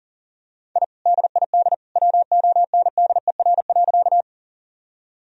Morse code, heard as a keyed tone, about 40 wpm, 700 Hz. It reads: I DID WONDER.
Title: Sentences from Top 1000 Words - Review Part 3 - 40wpm